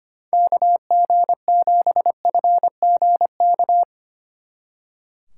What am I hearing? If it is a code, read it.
Morse code, heard as a keyed tone, about 32 wpm, 700 Hz. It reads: KG7FGK